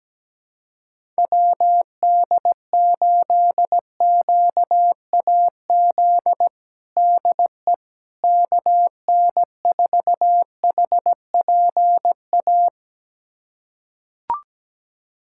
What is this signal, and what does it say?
Morse code, heard as a keyed tone, about 17 wpm, 700 Hz. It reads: WD8QAZ DE KN4HPA
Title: USA Callsign DE Callsign - 6 Characters - 17wpm